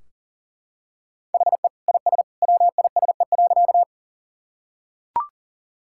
Morse code, heard as a keyed tone, about 40 wpm, 700 Hz. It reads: HE IS WISE.
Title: Sentences from Top 1600 Words 40wpm